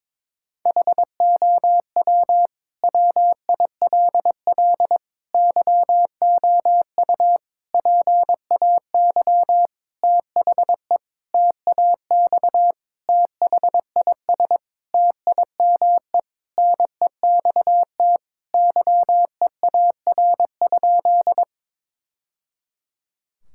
Morse code, heard as a keyed tone, about 22 wpm, 700 Hz. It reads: HOW WILL YOU PAY THE TAX THIS TIME NEXT YEAR?
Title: Sentences from Top 1000 Words - Review Part 5 - 22wpm